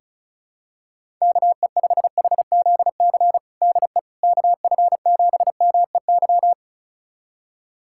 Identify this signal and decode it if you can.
Morse code, heard as a keyed tone, about 34 wpm, 700 Hz. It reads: KE5HZC DE KF7MEY
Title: USA Callsign DE Callsign - 6 Characters - 35wpm